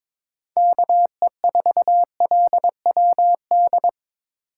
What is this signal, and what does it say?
Morse code, heard as a keyed tone, about 22 wpm, 700 Hz. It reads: KE4LWD
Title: USA Call Signs - 6 Characters - 2x Character Spacing - 22wpm